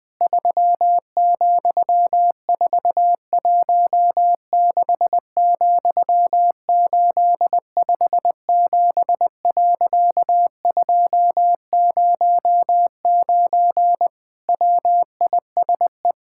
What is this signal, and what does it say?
Morse code, heard as a keyed tone, about 20 wpm, 700 Hz. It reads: 3,416,857.209 WISE
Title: Pangram Challenge v2 - 20wpm